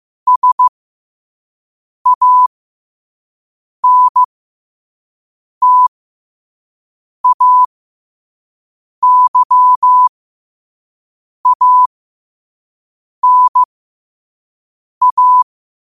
Morse code, heard as a keyed tone, about 15 wpm, spaced as for 5 wpm, 1000 Hz. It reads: SANTAYANA
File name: 5wpm